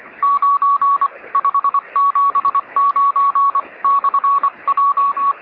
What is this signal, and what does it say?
Morse code, heard as a keyed tone, about 24 wpm, 1120 Hz. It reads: 9 579 /J
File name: sm7lcb / 1849z oz3zw 13cm